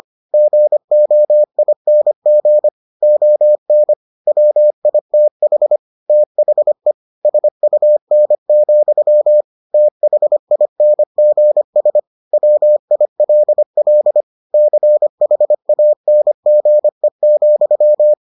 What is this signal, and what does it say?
Morse code, heard as a keyed tone, about 25 wpm, 600 Hz. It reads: GOING ON WITH THE SUN, THINGS WILL CHANGE,